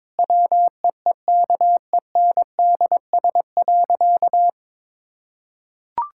Morse code, heard as a keyed tone, about 22 wpm, 700 Hz. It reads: WEEKENDS.
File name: Difficult Sentences with Hints - 22wpm